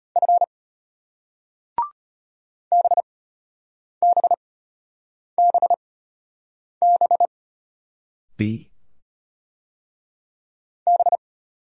Morse code, heard as a keyed tone, about 34 wpm, spaced as for 8 wpm, 700 Hz. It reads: F BBBB B